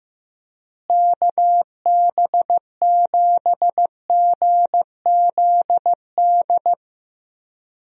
Morse code, heard as a keyed tone, about 15 wpm, 700 Hz. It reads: KB7GZD
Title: USA Call Signs - 6 Characters - Repeated 2x - 15wpm